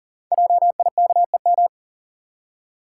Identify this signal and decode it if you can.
Morse code, heard as a keyed tone, about 39 wpm, 690 Hz. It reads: JIKEM
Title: Code Groups - 5 Letters - 40wpm